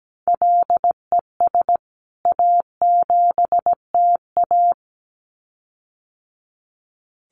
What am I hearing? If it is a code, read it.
Morse code, heard as a keyed tone, about 17 wpm, 700 Hz. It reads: LES A7TA
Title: CWT Exchange 17wpm